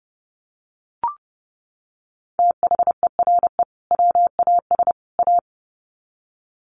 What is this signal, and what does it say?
Morse code, heard as a keyed tone, about 29 wpm, 700 Hz. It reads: THERE WAS A
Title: Sets of 3 Words 30wpm